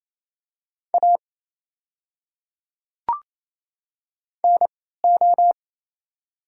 Morse code, heard as a keyed tone, about 28 wpm, spaced as for 16 wpm, 700 Hz. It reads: A NO
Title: 3x Character Spacing - Top 100 Words - 28wpm